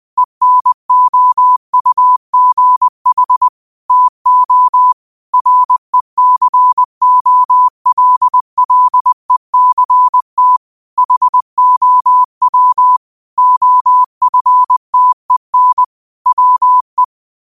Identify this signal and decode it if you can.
Morse code, heard as a keyed tone, about 20 wpm, 1000 Hz. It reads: ENOUGH TO RECOLLECT HOW OFTEN WE
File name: qotdfiles / 20wpm